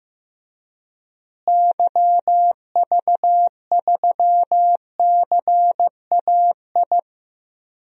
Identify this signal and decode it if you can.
Morse code, heard as a keyed tone, about 15 wpm, 700 Hz. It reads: YV3CAI
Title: DXCC-100 Call Signs - 6 Characters - Repeated 3x - 15wpm